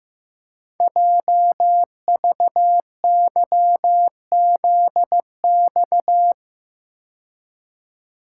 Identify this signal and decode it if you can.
Morse code, heard as a keyed tone, about 15 wpm, 700 Hz. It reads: JVYZX
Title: Code Groups - 5 Letters - 15wpm